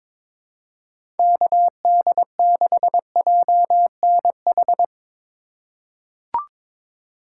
Morse code, heard as a keyed tone, about 22 wpm, 700 Hz. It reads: KD6JNH
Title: USA Call Signs - 6 Characters - 4x Character Spacing - 22wpm